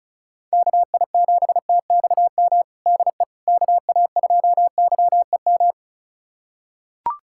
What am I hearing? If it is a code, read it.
Morse code, heard as a keyed tone, about 36 wpm, 700 Hz. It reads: KI7TXM DE KA2YEM
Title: USA Callsign DE Callsign - 6 Characters - 35wpm